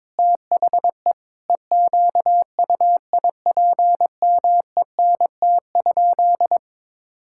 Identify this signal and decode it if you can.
Morse code, heard as a keyed tone, about 21 wpm, 700 Hz. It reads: THE EQUIPMENT?
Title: Sentences from Top 1000 Words - Review Part 6 - 22wpm